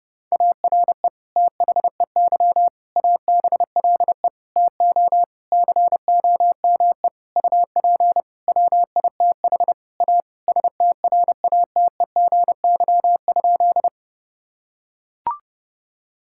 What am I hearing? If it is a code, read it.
Morse code, heard as a keyed tone, about 31 wpm, 700 Hz. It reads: ARE THEY ABLE TO COME UP WITH A STRATEGY?
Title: Sentences from Top 1000 Words - Review Part 5 - 30wpm